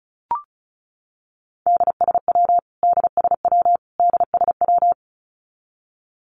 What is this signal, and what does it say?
Morse code, heard as a keyed tone, about 34 wpm, 700 Hz. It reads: DSW DSW DSW
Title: Abbreviations Repeated 3x - 35wpm